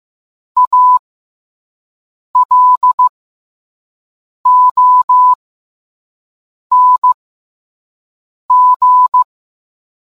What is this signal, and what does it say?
Morse code, heard as a keyed tone, about 15 wpm, spaced as for 5 wpm, 1000 Hz. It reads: ALONG